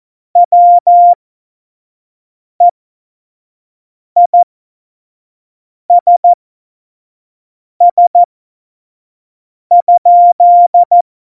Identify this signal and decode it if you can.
Morse code, heard as a keyed tone, about 14 wpm, spaced as for 5 wpm, 700 Hz. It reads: WEISS?
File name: Morsekurs / Witze / Witz 20. November 2018, 25 F70